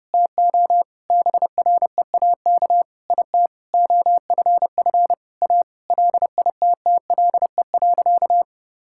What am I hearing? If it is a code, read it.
Morse code, heard as a keyed tone, about 30 wpm, 700 Hz. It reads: TO BREAK IT OFF A LITTLE.
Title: Sentences from Top 1000 Words - Review Part 6 - 30wpm